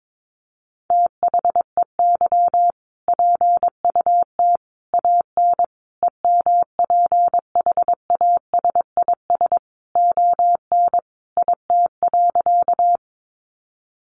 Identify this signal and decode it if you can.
Morse code, heard as a keyed tone, about 22 wpm, 700 Hz. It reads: THEY PUT AN EMPHASIS ON IT.
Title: Sentences from Top 1400 Words 22wpm